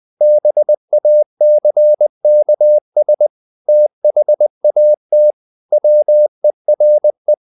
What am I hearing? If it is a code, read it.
Morse code, heard as a keyed tone, about 20 wpm, 600 Hz. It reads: BACKS THAT WERE
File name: the old man and the sea / the old man and the sea part 1 20wpm